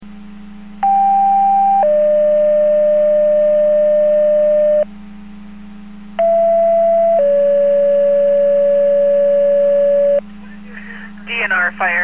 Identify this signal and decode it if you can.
Two-tone page, heard as a one-second tone, then a three-second tone, about 790 then 600 Hz, about 670 then 570 Hz, with speech. It is heard from 0.8 to 10.2 s.